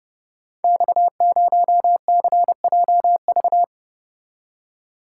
Morse code, heard as a keyed tone, about 28 wpm, 700 Hz. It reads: X0CJV
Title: Code Groups - 5 Letter:Numbers - 30wpm